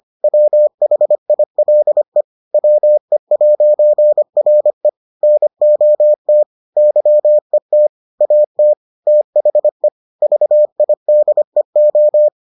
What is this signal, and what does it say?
Morse code, heard as a keyed tone, about 25 wpm, 600 Hz. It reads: WHILE WE'RE NOT YET AT THE VIDEO